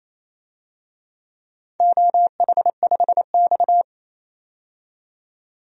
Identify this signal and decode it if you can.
Morse code, heard as a keyed tone, about 29 wpm, 700 Hz. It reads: OH5X
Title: DXCC-100 Call Signs - 4 Characters - 28wpm